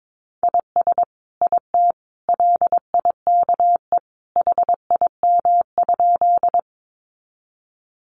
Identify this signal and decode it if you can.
Morse code, heard as a keyed tone, about 22 wpm, 700 Hz. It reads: IS IT LIKE HIM?
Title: Sentences from Top 100 Words 22wpm